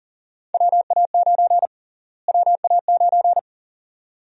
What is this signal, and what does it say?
Morse code, heard as a keyed tone, about 37 wpm, 700 Hz. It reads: WA9 WA9
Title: USA Call Sign Prefixes - Repeated 2x - 40wpm